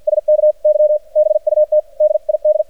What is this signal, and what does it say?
Morse code, heard as a keyed tone, about 53 wpm, spaced as for 45 wpm, 610 Hz, with radio noise. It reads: SK Y BUT DID
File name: archived / CRSnet